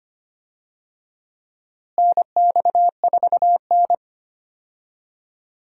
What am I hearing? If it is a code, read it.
Morse code, heard as a keyed tone, about 25 wpm, 700 Hz. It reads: NX4N
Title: USA Call Signs - 4 Characters - Repeated 3x - 25wpm